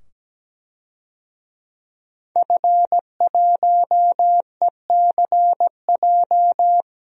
Morse code, heard as a keyed tone, about 17 wpm, 700 Hz. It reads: F1ECJ
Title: DXCC-100 Call Signs - 5 Characters - 17wpm